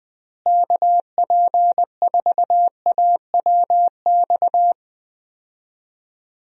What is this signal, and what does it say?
Morse code, heard as a keyed tone, about 20 wpm, 700 Hz. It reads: KP4AWX
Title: DXCC-100 Call Signs - 6 Characters - Repeated 3x - 20wpm